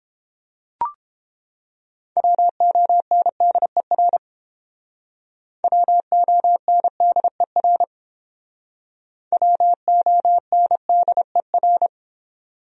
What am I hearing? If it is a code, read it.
Morse code, heard as a keyed tone, about 30 wpm, 700 Hz. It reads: WONDER WONDER WONDER